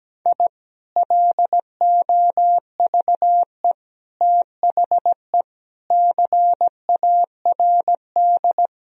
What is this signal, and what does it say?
Morse code, heard as a keyed tone, about 17 wpm, 700 Hz. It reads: I LOVE THE CARD
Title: Sentences from Top 1000 Words - Review Part 4 - 17wpm